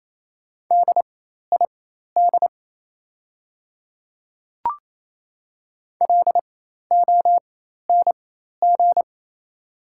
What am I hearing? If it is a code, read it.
Morse code, heard as a keyed tone, about 28 wpm, spaced as for 13 wpm, 700 Hz. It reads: DID LONG
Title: 4x Character Spacing - Top 100 Words - 28wpm